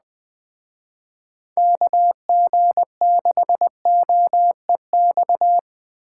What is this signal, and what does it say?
Morse code, heard as a keyed tone, about 20 wpm, 700 Hz. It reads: KG6OEX